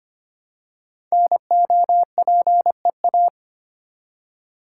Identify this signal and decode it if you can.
Morse code, heard as a keyed tone, about 25 wpm, 690 Hz. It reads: NOPEA
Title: Code Groups - 5 Letters - 25wpm